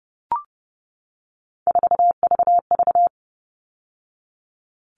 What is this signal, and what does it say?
Morse code, heard as a keyed tone, about 30 wpm, 700 Hz. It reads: 4VV
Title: All Cut Numbers - 3-Digit 30wpm